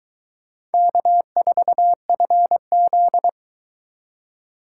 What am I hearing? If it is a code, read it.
Morse code, heard as a keyed tone, about 22 wpm, 700 Hz. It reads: K4FZ